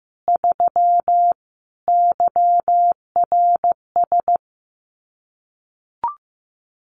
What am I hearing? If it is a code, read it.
Morse code, heard as a keyed tone, about 15 wpm, 700 Hz. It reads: 3 YRS
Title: Common QSO Phrases Encore 15wpm